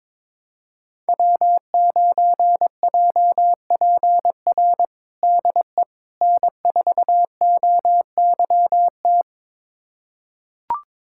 Morse code, heard as a keyed tone, about 22 wpm, 700 Hz. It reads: W9JPR DE N4OYT
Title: USA Callsign DE Callsign - 5 Characters - 22wpm